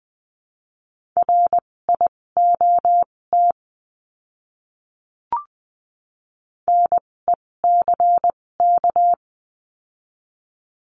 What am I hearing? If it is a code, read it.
Morse code, heard as a keyed tone, about 20 wpm, spaced as for 16 wpm, 700 Hz. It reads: RIOT NECK
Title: 256 Lesson 42a - Words 12-20wpm